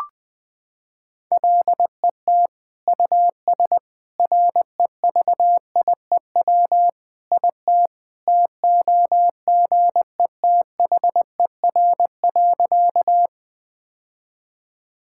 Morse code, heard as a keyed tone, about 20 wpm, 700 Hz. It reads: LET US REVIEW IT TOGETHER.
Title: Sentences from Top 1000 Words - Review Part 6 - 20wpm